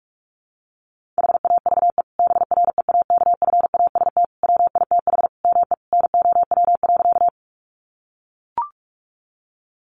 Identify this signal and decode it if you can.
Morse code, heard as a keyed tone, about 44 wpm, 700 Hz. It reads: HAVE BREAKFAST WITH ME NOW.
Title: Sentences from Top 1500 Words 45wpm